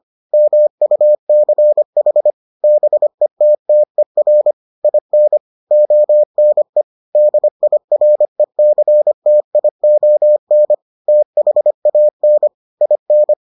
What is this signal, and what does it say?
Morse code, heard as a keyed tone, about 25 wpm, 600 Hz. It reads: MUCH BETTER IN ONE DIRECTION THAN IN